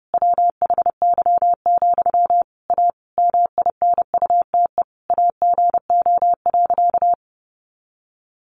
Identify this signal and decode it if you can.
Morse code, heard as a keyed tone, about 30 wpm, 700 Hz. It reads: WHY, A MINUTE AGO.
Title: Sentences from Top 500 Words 30wpm